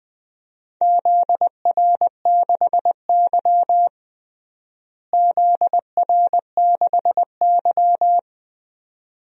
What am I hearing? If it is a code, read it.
Morse code, heard as a keyed tone, about 21 wpm, 700 Hz. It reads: ZR6Y ZR6Y
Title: DXCC-100 Call Signs - 4 Characters - Repeated 2x - 20wpm